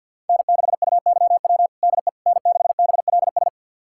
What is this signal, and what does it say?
Morse code, heard as a keyed tone, about 46 wpm, 700 Hz. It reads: N6UYW DE N6BLS
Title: USA Callsign DE Callsign - 5 Characters - 50wpm